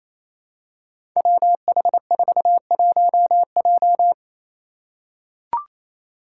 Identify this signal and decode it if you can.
Morse code, heard as a keyed tone, about 28 wpm, 700 Hz. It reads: WH41J
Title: Five Challenging Letter-Number Combinations 28wpm